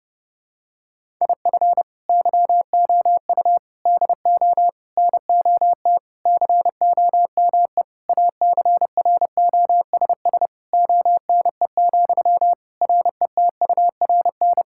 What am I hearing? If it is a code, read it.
Morse code, heard as a keyed tone, about 30 wpm, 700 Hz. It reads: IF YOU DO NOT COME ACROSS ONE, RETURN
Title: Sentences from Top 1000 Words - Review Part 5 - 30wpm